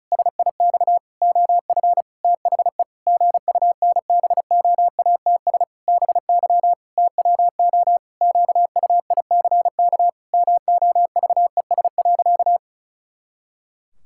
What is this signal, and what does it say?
Morse code, heard as a keyed tone, about 36 wpm, 700 Hz. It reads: SIX OF THE GUNBOATS BY TWO QUICK MOVES.
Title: Pangram Challenge v2 - 35wpm